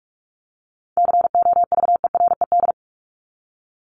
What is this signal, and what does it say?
Morse code, heard as a keyed tone, about 44 wpm, 700 Hz. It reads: COVERED
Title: ED Suffix - Top 100 Words with Base Words 45wpm